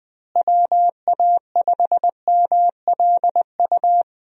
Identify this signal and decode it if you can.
Morse code, heard as a keyed tone, about 21 wpm, 700 Hz. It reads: WA5MLU
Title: USA Call Signs - 6 Characters - Repeated 3x - 20wpm